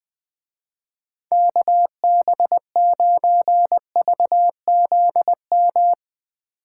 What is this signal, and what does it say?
Morse code, heard as a keyed tone, about 20 wpm, 700 Hz. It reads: KB9VZM
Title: Speed Racing - USA 2x3 Call Signs 20wpm